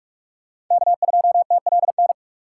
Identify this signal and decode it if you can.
Morse code, heard as a keyed tone, about 46 wpm, 700 Hz. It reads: K2TLN
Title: USA Call Signs - 5 Characters - 3x Character Spacing - 45wpm